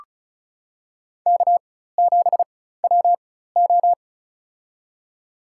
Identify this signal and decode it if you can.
Morse code, heard as a keyed tone, about 34 wpm, spaced as for 16 wpm, 700 Hz. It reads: K7WO
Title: USA Call Signs - 4 Characters - 4x Character Spacing - 35wpm